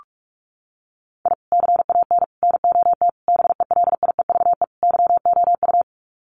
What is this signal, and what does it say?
Morse code, heard as a keyed tone, about 44 wpm, 700 Hz. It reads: I CAN NOT BELIEVE YOU
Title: Sets of 5 Words Encore 45wpm